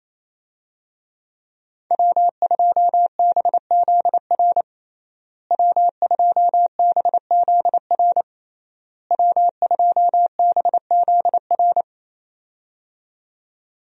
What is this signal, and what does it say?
Morse code, heard as a keyed tone, about 28 wpm, 700 Hz. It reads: W2BZR W2BZR W2BZR
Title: USA Call Signs - 5 Characters - Repeated 3x - 28wpm